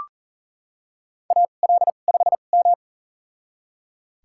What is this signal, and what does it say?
Morse code, heard as a keyed tone, about 40 wpm, spaced as for 29 wpm, 700 Hz. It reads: AL5M
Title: USA Call Signs - 4 Characters - 2x Character Spacing - 40wpm